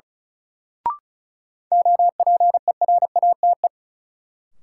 Morse code, heard as a keyed tone, about 36 wpm, 700 Hz. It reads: OPERATE
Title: Sentences from Top 1500 Words - New Word Drill - 35wpm